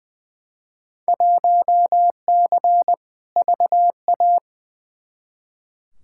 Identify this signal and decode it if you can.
Morse code, heard as a keyed tone, about 20 wpm, 700 Hz. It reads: 1C VA
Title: Field Day Exchanges Encore - 20wpm